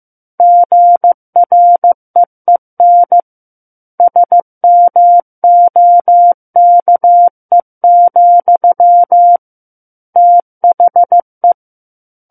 Morse code, heard as a keyed tone, about 15 wpm, 700 Hz. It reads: GREEN SMOKE, THE
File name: Chapter 6 - 15 WPM